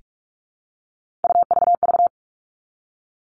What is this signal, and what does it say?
Morse code, heard as a keyed tone, about 44 wpm, 700 Hz. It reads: UVV
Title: All Cut Numbers - 3-Digit 45wpm